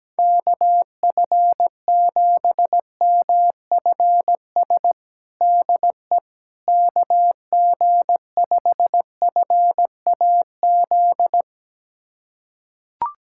Morse code, heard as a keyed tone, about 17 wpm, 700 Hz. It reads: KF7MFS DE KG5FAZ
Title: USA Callsign DE Callsign - 6 Characters - 17wpm